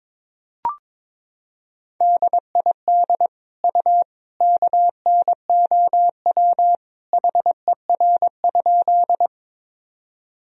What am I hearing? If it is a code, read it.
Morse code, heard as a keyed tone, about 22 wpm, 700 Hz. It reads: DID U KNOW HER?
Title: Sentences from Top 100 Words and Abbreviations 22wpm